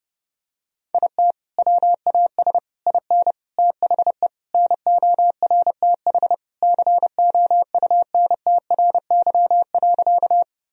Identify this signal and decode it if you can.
Morse code, heard as a keyed tone, about 32 wpm, 700 Hz. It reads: IT WAS IN THE NORTH COUNTRY.
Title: Sentences from Top 1000 Words - Review Part 1 - 30wpm